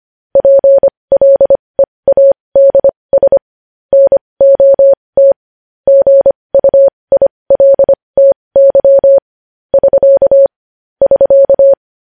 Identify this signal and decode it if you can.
Morse code, heard as a keyed tone, about 25 wpm, spaced as for 20 wpm, 550 Hz. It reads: PLEADS NOT GUILTY <SK> <SK>
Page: news-headlines-in-morse-code-at-20-wpm